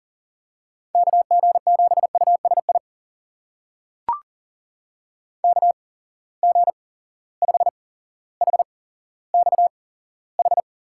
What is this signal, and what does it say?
Morse code, heard as a keyed tone, about 40 wpm, spaced as for 17 wpm, 700 Hz. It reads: KG7USI KG5HXH
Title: USA Call Signs - 8x Character Spacing - 40wpm